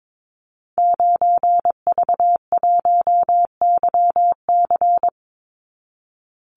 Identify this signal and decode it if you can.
Morse code, heard as a keyed tone, about 22 wpm, 700 Hz. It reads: 9V1YC
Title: Call Signs - 22wpm